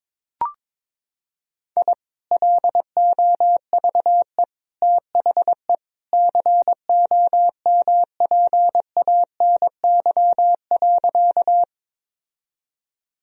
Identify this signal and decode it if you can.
Morse code, heard as a keyed tone, about 22 wpm, 700 Hz. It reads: I LOVE THE COMPANY.
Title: Sentences from Top 1000 Words - Review Part 3 - 22wpm